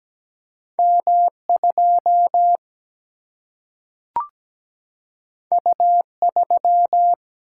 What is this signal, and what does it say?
Morse code, heard as a keyed tone, about 17 wpm, 700 Hz. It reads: M2 U3
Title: Single-Letter Single-Number 17wpm